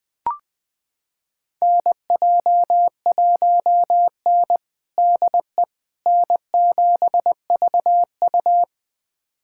Morse code, heard as a keyed tone, about 21 wpm, 700 Hz. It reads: NJ1N DE N7VU
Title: USA Callsign DE Callsign - 4 Characters - 20wpm